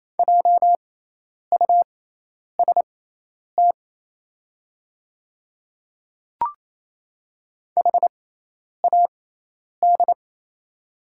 Morse code, heard as a keyed tone, about 28 wpm, spaced as for 10 wpm, 690 Hz. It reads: JUST HAD
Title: 6x Character Spacing - Top 300 Words - 28wpm